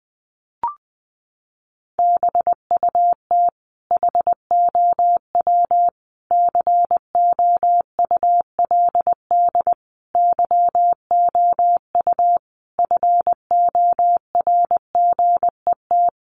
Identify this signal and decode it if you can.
Morse code, heard as a keyed tone, about 20 wpm, 700 Hz. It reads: BUT HOW COULD YOU FORGET
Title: Sentences from Top 1200 Words 20wpm